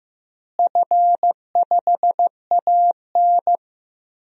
Morse code, heard as a keyed tone, about 15 wpm, 700 Hz. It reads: F5AN
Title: DXCC-100 Call Signs - 4 Characters - Repeated 3x - 15wpm